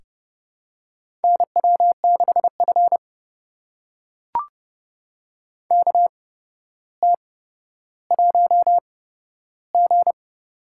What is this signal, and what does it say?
Morse code, heard as a keyed tone, about 32 wpm, spaced as for 13 wpm, 700 Hz. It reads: NW6F KT1G
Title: USA Call Signs - 4 Characters - 8x Character Spacing - 30wpm